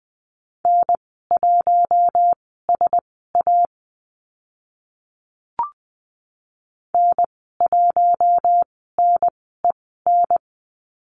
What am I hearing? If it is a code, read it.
Morse code, heard as a keyed tone, about 20 wpm, spaced as for 14 wpm, 700 Hz. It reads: N1SA N1NEN
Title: 042 Lesson 8g - Number 1 - Call Signs 20wpm